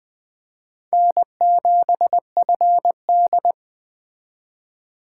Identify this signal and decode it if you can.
Morse code, heard as a keyed tone, about 20 wpm, 700 Hz. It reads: N7FD
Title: Speed Racing - USA 1x2 Call Signs 17wpm